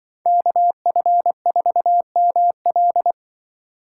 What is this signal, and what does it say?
Morse code, heard as a keyed tone, about 24 wpm, 700 Hz. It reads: KF4ML
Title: Speed Racing - USA 2x2 Call Signs 20wpm